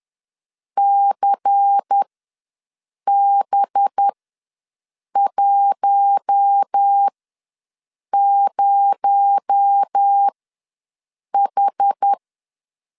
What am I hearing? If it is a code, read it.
Morse code, heard as a keyed tone, about 10 wpm, spaced as for 6 wpm, 800 Hz. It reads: CB10H
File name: LEZIONI / 8 YJ45 / 8 YJ45 ws 10wpm 3843